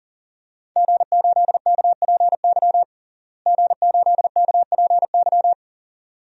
Morse code, heard as a keyed tone, about 37 wpm, 690 Hz. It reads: G8KPY G8KPY
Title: DXCC-100 Call Signs - 5 Characters - Repeated 2x - 40wpm